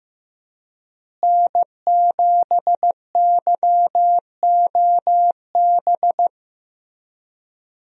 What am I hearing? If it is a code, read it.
Morse code, heard as a keyed tone, about 15 wpm, 700 Hz. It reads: N7YOB
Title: USA Call Signs - 5 Characters - Repeated 2x - 15wpm